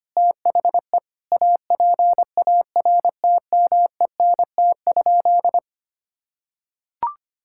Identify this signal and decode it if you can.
Morse code, heard as a keyed tone, about 25 wpm, 700 Hz. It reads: THE APARTMENT?
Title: Sentences from Top 1000 Words - Review Part 6 - 25wpm